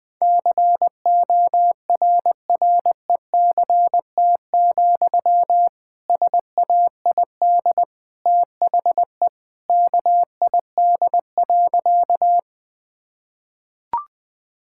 Morse code, heard as a keyed tone, about 20 wpm, 700 Hz. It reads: CORRECT, SAID THE KID.
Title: Sentences from Top 1600 Words 20wpm